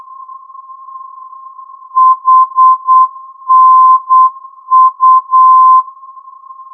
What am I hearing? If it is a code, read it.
Morse code, heard as a keyed tone, about 8 wpm, 1040 Hz. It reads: HNU
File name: ndb hanau hnu-432